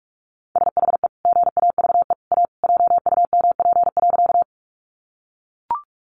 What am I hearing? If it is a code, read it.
Morse code, heard as a keyed tone, about 44 wpm, 700 Hz. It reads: SHE GAVE A JUMP.
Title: Sentences from Top 1300 Words 45wpm